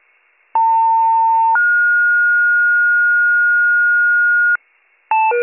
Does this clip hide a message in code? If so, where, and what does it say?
Yes: two-tone page from 0.5 to 4.6 s, heard as a one-second tone, then a three-second tone, about 900 then 1430 Hz, with noise.